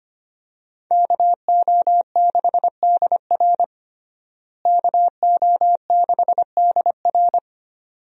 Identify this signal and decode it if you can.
Morse code, heard as a keyed tone, about 25 wpm, 700 Hz. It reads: KO6DR KO6DR
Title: USA Call Signs - 5 Characters - Repeated 2x - 25wpm